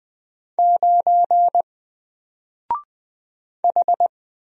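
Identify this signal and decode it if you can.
Morse code, heard as a keyed tone, about 20 wpm, 700 Hz. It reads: 9 H
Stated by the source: Single letter and number Morse drill at 20 wpm, from N4LCD.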